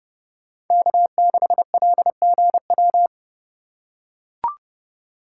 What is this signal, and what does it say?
Morse code, heard as a keyed tone, about 29 wpm, 700 Hz. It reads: K6LGW